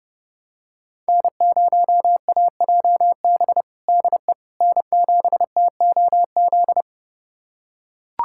Morse code, heard as a keyed tone, about 30 wpm, 700 Hz. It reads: N0AJB DE N7TOZ
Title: USA Callsign DE Callsign - 5 Characters - 30wpm